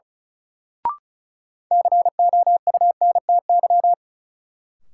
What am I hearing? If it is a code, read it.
Morse code, heard as a keyed tone, about 36 wpm, 700 Hz. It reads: COUNTY
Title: Sentences from Top 1200 Words - New Word Drill Alt - 35wpm